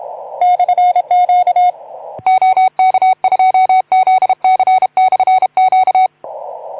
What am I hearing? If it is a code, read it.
Morse code, heard as a keyed tone, about 32 wpm, 750 Hz. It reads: /Q OK2ZC/Q
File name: Archiv / CUC181 / OK2ZC CUC 181